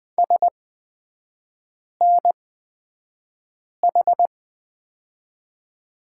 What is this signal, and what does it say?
Morse code, heard as a keyed tone, about 20 wpm, spaced as for 5 wpm, 700 Hz. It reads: SNH
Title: 224 Lesson 36e - BT - Three Characters 20wpm